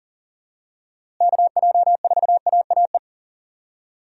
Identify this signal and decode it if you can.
Morse code, heard as a keyed tone, about 40 wpm, 700 Hz. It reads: KJ4AAE
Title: USA Call Signs - 6 Characters - Repeated 2x - 40wpm